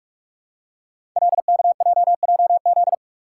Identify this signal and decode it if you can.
Morse code, heard as a keyed tone, about 44 wpm, 700 Hz. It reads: LKJJZ
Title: Code Groups - 5 Letters - 45wpm